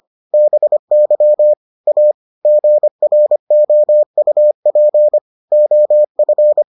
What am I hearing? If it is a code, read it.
Morse code, heard as a keyed tone, about 25 wpm, 600 Hz. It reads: BY A GROUP OF